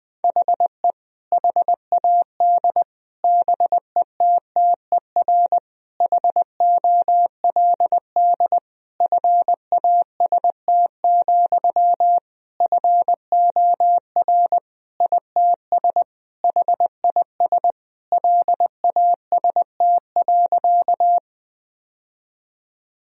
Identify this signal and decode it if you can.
Morse code, heard as a keyed tone, about 20 wpm, 700 Hz. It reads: HE HAD BETTER HOLD FAST, FOR ITS HIS LAST.
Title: Sentences from Top 1000 Words - Review Part 2 - 20wpm